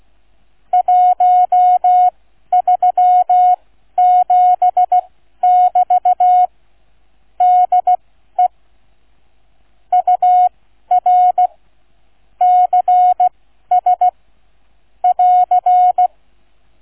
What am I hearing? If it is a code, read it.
Morse code, heard as a keyed tone, about 16 wpm, spaced as for 12 wpm, 730 Hz. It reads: 137<BT> DE UR CS <AR>